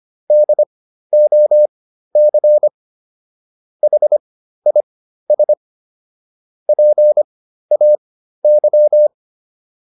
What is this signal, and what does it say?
Morse code, heard as a keyed tone, about 25 wpm, spaced as for 13 wpm, 600 Hz. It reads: DOC HIS PAY